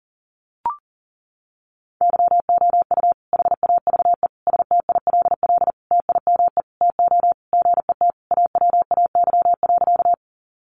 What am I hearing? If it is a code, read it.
Morse code, heard as a keyed tone, about 40 wpm, 700 Hz. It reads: YOU HAVE STILL TIME TO GET AWAY.
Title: Sentences from Top 600 Words 40wpm